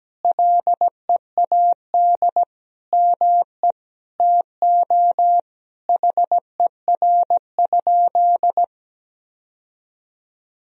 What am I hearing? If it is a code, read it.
Morse code, heard as a keyed tone, about 17 wpm, 700 Hz. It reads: LEAD ME TO HER?
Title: Sentences from Top 1000 Words - Review Part 2 - 17wpm